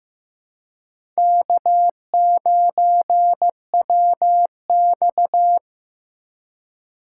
Morse code, heard as a keyed tone, about 15 wpm, 700 Hz. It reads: K9WX
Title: USA Call Signs - 4 Characters - 4x Character Spacing - 15wpm